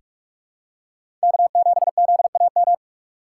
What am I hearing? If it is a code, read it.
Morse code, heard as a keyed tone, about 48 wpm, 700 Hz. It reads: K7ZAM
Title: USA Call Signs - 5 Characters - 4x Character Spacing - 45wpm